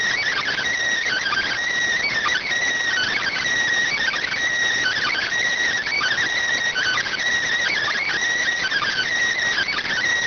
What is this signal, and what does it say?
Emergency Alert System (SAME) tones (header bursts), heard as harsh buzzing bursts. It is heard from 0.9 to 9.2 s.